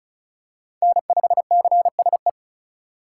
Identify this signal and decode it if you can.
Morse code, heard as a keyed tone, about 36 wpm, 700 Hz. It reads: N5CSE
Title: USA Call Signs - 5 Characters - 4x Character Spacing - 35wpm